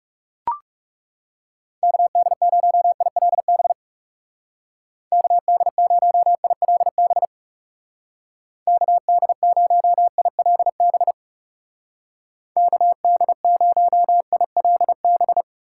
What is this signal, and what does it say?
Morse code, heard as a keyed tone, about 35 wpm, 700 Hz. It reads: KD0ILB KD0ILB KD0ILB KD0ILB
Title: Speed Racing - USA 2x3 Call Signs 30wpm